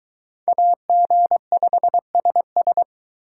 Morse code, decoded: AG5SS

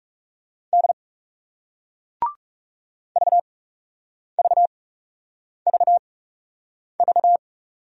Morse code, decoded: D VVVV

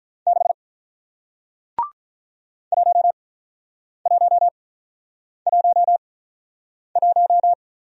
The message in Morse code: 6 1111

